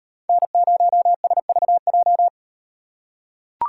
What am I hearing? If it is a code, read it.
Morse code, heard as a keyed tone, about 38 wpm, 700 Hz. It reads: N0SVJ